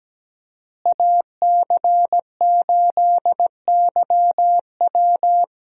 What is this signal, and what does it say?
Morse code, heard as a keyed tone, about 17 wpm, 700 Hz. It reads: AC8YW
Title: USA Call Signs - 5 Characters - Repeated 2x - 17wpm